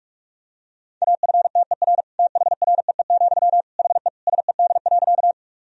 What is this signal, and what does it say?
Morse code, heard as a keyed tone, about 46 wpm, 700 Hz. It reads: AFTER THREE, HE SED.
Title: Sentences from Top 200 Words and Abbreviations 45wpm